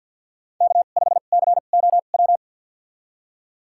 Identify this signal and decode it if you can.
Morse code, heard as a keyed tone, about 52 wpm, spaced as for 36 wpm, 700 Hz. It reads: K5COW
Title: USA Call Signs - 2x Character Spacing - 50wpm